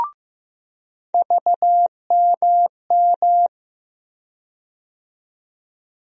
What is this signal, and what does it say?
Morse code, heard as a keyed tone, about 15 wpm, 700 Hz. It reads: VMM
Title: USA Call Sign Postfixes - 3 Letters 15wpm